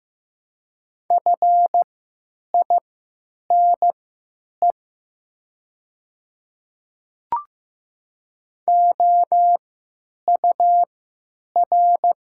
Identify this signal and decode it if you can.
Morse code, heard as a keyed tone, about 15 wpm, spaced as for 9 wpm, 700 Hz. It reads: FINE OUR
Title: 3x Character Spacing - Top 500 Words - 15wpm